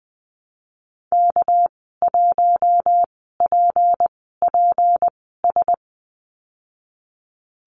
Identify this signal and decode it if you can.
Morse code, heard as a keyed tone, about 20 wpm, spaced as for 14 wpm, 700 Hz. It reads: K1PPS
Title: 219 Lesson 35g - Number 0 - Call Signs 20wpm